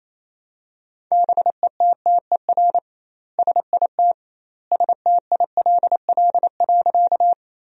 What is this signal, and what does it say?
Morse code, heard as a keyed tone, about 28 wpm, 700 Hz. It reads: BETTER SIT STILL.
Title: 2x Word-Spacing - Sentences from Top 500 Words - 28wpm